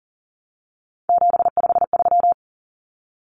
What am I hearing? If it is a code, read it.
Morse code, heard as a keyed tone, about 40 wpm, 700 Hz. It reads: 753
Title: Common Cut Numbers - 3-Digit 40wpm